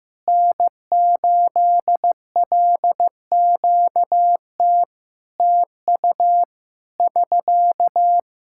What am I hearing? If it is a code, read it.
Morse code, heard as a keyed tone, about 15 wpm, 700 Hz. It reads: N8LQT TU <SK>
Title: Practice QSO - Exchange 4 - Ending - 15wpm